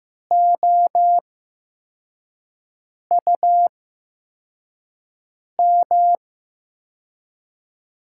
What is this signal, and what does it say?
Morse code, heard as a keyed tone, about 15 wpm, spaced as for 4 wpm, 700 Hz. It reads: OUM
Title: USA Call Signs - 5 Characters - 8x Character Spacing - 15wpm